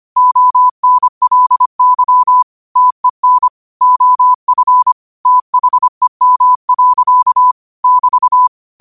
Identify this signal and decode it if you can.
Morse code, heard as a keyed tone, about 25 wpm, 1000 Hz. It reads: ONLY TEN OF THEM. <BT>
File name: qotdfiles / 25wpm